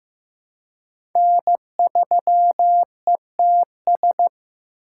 Morse code, heard as a keyed tone, about 15 wpm, 700 Hz. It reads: N3ETS
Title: USA Call Signs - 6x Character Spacing - 15wpm